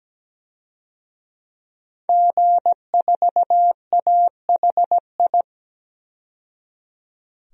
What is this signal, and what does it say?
Morse code, heard as a keyed tone, about 17 wpm, 700 Hz. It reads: G4AHI
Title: DXCC-100 Call Signs - 5 Characters - 17wpm